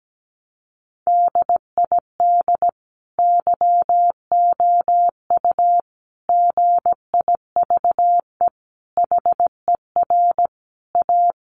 Morse code, heard as a keyed tone, about 17 wpm, 700 Hz. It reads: DID YOU GIVE HER A
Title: Sentences from Top 1500 Words 17wpm